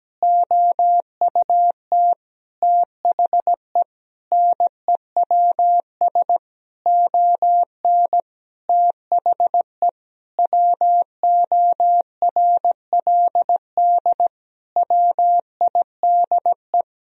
Morse code, heard as a keyed tone, about 17 wpm, 700 Hz. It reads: OUT THE NEWS ON THE WORLD WIDE
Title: Sentences from Top 1000 Words - Review Part 4 - 17wpm